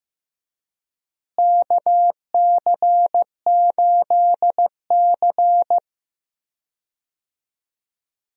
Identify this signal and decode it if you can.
Morse code, heard as a keyed tone, about 16 wpm, 700 Hz. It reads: KC8C
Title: USA Call Signs - 4 Characters - Repeated 3x - 15wpm